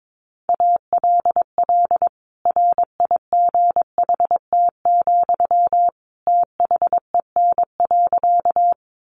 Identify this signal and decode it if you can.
Morse code, heard as a keyed tone, about 22 wpm, 700 Hz. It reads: ALL RIGHT, THEN.
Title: Sentences from Top 200 Words 22wpm